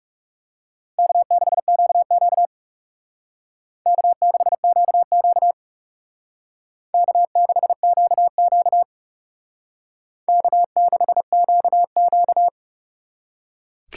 Morse code, heard as a keyed tone, about 35 wpm, 700 Hz. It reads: K6QQ K6QQ K6QQ K6QQ
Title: Speed Racing - USA 1x2 Call Signs 30wpm